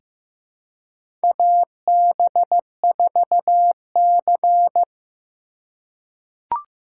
Morse code, heard as a keyed tone, about 15 wpm, 700 Hz. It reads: AB4C
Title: USA Call Signs - 4x Character Spacing - 15wpm